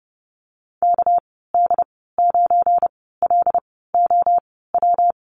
Morse code, heard as a keyed tone, about 30 wpm, spaced as for 17 wpm, 700 Hz. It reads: KD9LOW
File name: Lesson 37g - Letter x - Call Signs 30wpm